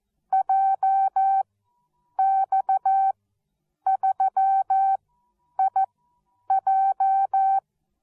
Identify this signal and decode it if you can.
Morse code, heard as a keyed tone, about 14 wpm, spaced as for 8 wpm, 770 Hz. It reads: JX3IJ